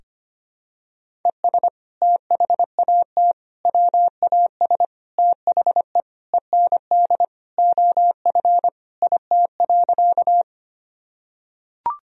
Morse code, heard as a keyed tone, about 25 wpm, 700 Hz. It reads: ES THAT WAS THE END OF IT.
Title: Sentences from Top 200 Words and Abbreviations 25wpm